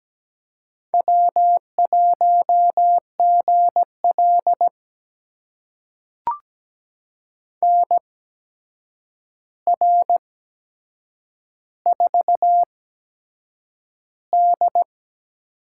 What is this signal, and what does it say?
Morse code, heard as a keyed tone, about 17 wpm, spaced as for 7 wpm, 700 Hz. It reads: W1GL NR4D
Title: USA Call Signs - 4 Characters - 8x Character Spacing - 17wpm